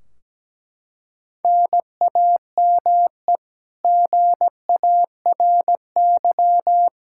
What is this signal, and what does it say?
Morse code, heard as a keyed tone, about 17 wpm, 700 Hz. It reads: NAME GARY
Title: Practice QSO - Exchange 1 - Beginning - 17wpm